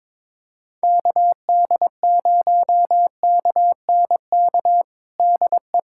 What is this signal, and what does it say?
Morse code, heard as a keyed tone, about 22 wpm, 700 Hz. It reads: KD0KNK DE
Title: USA Callsign DE Callsign - 6 Characters - 22wpm